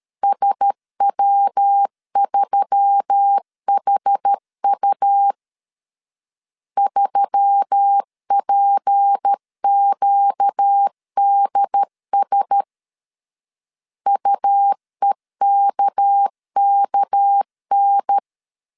Morse code, heard as a keyed tone, about 13 wpm, 800 Hz. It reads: SW3HU 3PQDS UEKKN
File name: LEZIONI / 6 PKX0 / 6 PK 12wpm 2006